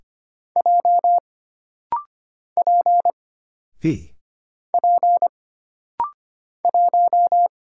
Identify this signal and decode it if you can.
Morse code, heard as a keyed tone, about 25 wpm, 700 Hz. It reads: J P P 1
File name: Single Character (Letter:Number:Punctuation:Prosign) - Rapid-Fire - 25wpm